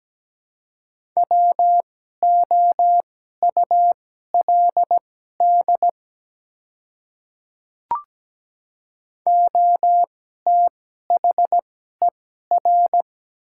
Morse code, heard as a keyed tone, about 17 wpm, spaced as for 12 wpm, 690 Hz. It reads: WOULD OTHER
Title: 2x Character Spacing - Top 100 Words - 17wpm